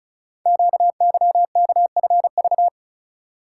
Morse code, heard as a keyed tone, about 36 wpm, 700 Hz. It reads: QYKFV
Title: Code Groups - 5 Letters - 35wpm